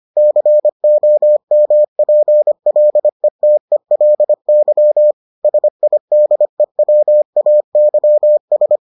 Morse code, heard as a keyed tone, about 25 wpm, 600 Hz. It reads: COMPLETELY SIDEWAYS